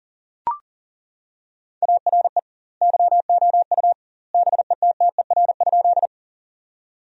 Morse code, heard as a keyed tone, about 40 wpm, 700 Hz. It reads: ARE YOU BETTER?
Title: 2x Word-Spacing - Sentences from Top 400 Words - 40wpm